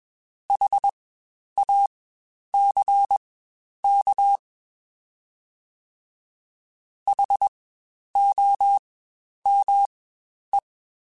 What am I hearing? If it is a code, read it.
Morse code, heard as a keyed tone, about 21 wpm, spaced as for 10 wpm, 800 Hz. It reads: HACK HOME
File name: Spaced Alphas / Spaced Alphas mp3s / Hs sp @20